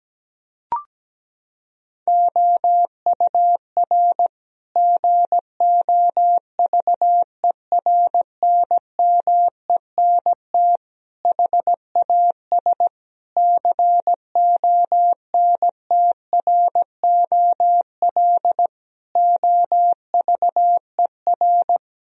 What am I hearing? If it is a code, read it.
Morse code, heard as a keyed tone, about 17 wpm, 700 Hz. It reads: OUR GOVERNMENT HAS CONTROL OVER
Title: Sentences from Top 1000 Words - Review Part 4 - 17wpm